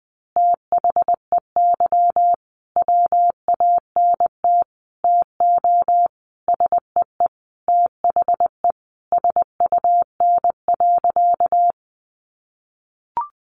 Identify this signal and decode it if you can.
Morse code, heard as a keyed tone, about 20 wpm, 700 Hz. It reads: THEY WANT TO SEE THE SUN.
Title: Easy Phrases 20wpm